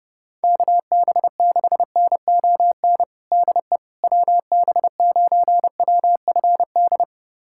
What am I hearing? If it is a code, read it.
Morse code, heard as a keyed tone, about 30 wpm, 700 Hz. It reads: KB6NON DE WB9WFD
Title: USA Callsign DE Callsign 30wpm